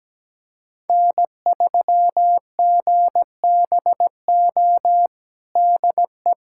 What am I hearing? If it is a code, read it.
Morse code, heard as a keyed tone, about 17 wpm, 700 Hz. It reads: N3GBO DE
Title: USA Callsign DE Callsign - 5 Characters - 17wpm